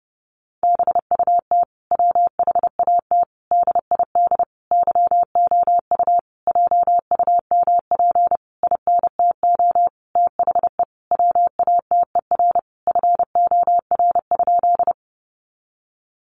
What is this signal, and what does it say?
Morse code, heard as a keyed tone, about 30 wpm, 700 Hz. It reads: BUT WHAT DID YOU JUMP INTO THE WATER FOR?
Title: Sentences from Top 1300 Words 30wpm